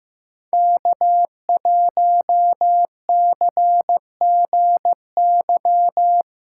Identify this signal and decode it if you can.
Morse code, heard as a keyed tone, about 18 wpm, 700 Hz. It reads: K1CGY
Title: Speed Racing - USA 1x3 Call Signs 15wpm